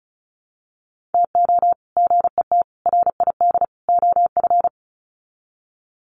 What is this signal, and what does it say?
Morse code, heard as a keyed tone, about 34 wpm, 700 Hz. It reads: TO GET RID OF
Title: Sets of 4 Words 35wpm